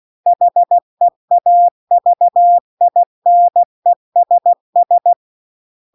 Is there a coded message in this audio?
Yes: Morse code from 0.3 to 5.1 s, heard as a keyed tone, about 16 wpm, 700 Hz. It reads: HEAVINESS